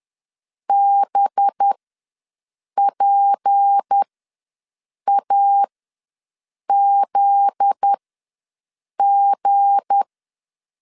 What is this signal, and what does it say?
Morse code, heard as a keyed tone, about 10 wpm, spaced as for 6 wpm, 800 Hz. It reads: BPAZG
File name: LEZIONI / 10 SIGNS / 10B INT ws 10wpm 4134